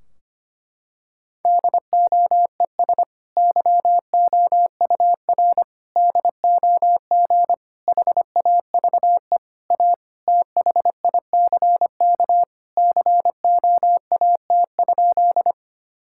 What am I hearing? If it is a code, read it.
Morse code, heard as a keyed tone, about 25 wpm, 700 Hz. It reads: DOES YOUR DOG HAVE A THICK COAT?
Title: Sentences from Top 1600 Words 25wpm